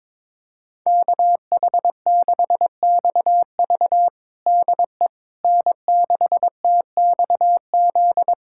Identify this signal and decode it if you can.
Morse code, heard as a keyed tone, about 22 wpm, 700 Hz. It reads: KH6XV DE N6TXZ
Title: USA Callsign DE Callsign - 5 Characters - 22wpm